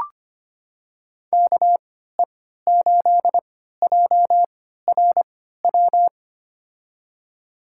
Morse code, heard as a keyed tone, about 25 wpm, spaced as for 14 wpm, 700 Hz. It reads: KE8JRW
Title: USA Call Signs - 3x Character Spacing - 25wpm